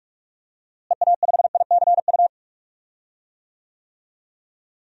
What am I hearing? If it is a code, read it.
Morse code, heard as a keyed tone, about 46 wpm, 700 Hz. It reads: EA5ICU
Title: DXCC-100 Call Signs - 6 Characters - Repeated 3x - 45wpm